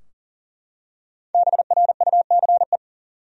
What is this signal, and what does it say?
Morse code, heard as a keyed tone, about 40 wpm, 700 Hz. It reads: BRUCE
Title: USA Common Names - 40wpm